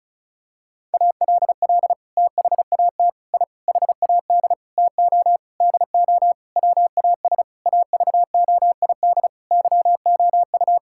Morse code, heard as a keyed tone, about 36 wpm, 700 Hz. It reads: ALL THAT I HAD TO DO WAS AVOID YOU
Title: Sentences from Top 1000 Words - Review Part 6 - 35wpm